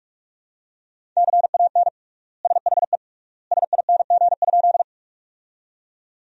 Morse code, heard as a keyed tone, about 46 wpm, 700 Hz. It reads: CAN SHE SING?
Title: 3x Word-Spacing - Sentences from Top 400 Words - 45wpm